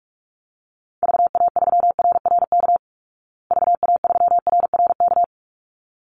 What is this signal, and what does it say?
Morse code, heard as a keyed tone, about 44 wpm, 700 Hz. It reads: VA3RRK VA3RRK
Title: Call Signs Repeated 3x - 45wpm